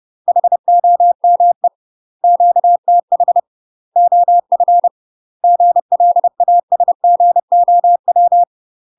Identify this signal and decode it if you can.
Morse code, heard as a keyed tone, about 30 wpm, 700 Hz. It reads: HOME QTH OF GLASGOW
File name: UK MASTER QSOs TRAINING / 30WPM / UK MASTER QSOs 0050